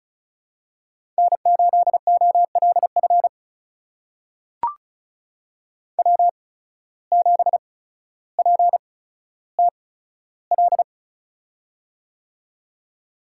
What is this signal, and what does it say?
Morse code, heard as a keyed tone, about 34 wpm, spaced as for 15 wpm, 700 Hz. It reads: N8OLF W7PTL
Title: USA Call Signs - 5 Characters - 8x Character Spacing - 35wpm